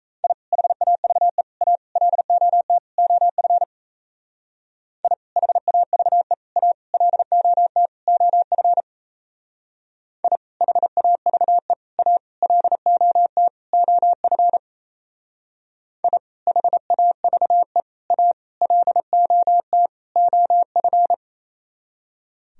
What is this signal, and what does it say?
Morse code, heard as a keyed tone, about 34 wpm, 700 Hz. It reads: I HAVE A LOT OF I HAVE A LOT OF I HAVE A LOT OF I HAVE A LOT OF
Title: Speed-Racing - Sets of 5 Words 28wpm